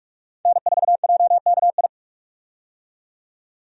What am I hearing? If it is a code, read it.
Morse code, heard as a keyed tone, about 46 wpm, 700 Hz. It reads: N4JKI
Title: USA Call Signs - 5 Characters - 45wpm